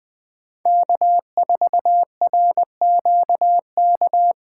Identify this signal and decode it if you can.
Morse code, heard as a keyed tone, about 20 wpm, 700 Hz. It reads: K4RQK